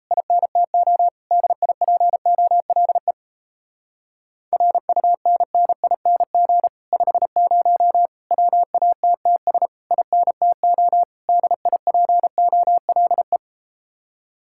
Morse code, heard as a keyed tone, about 38 wpm, 700 Hz. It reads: INTO DIPOLE RUNNING 50 WATTS INTO DIPOLE